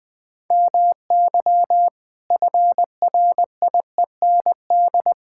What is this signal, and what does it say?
Morse code, heard as a keyed tone, about 20 wpm, 700 Hz. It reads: MY FRIEND
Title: Sentences with 5-Word Phrases - Spoken Beginning - 20wpm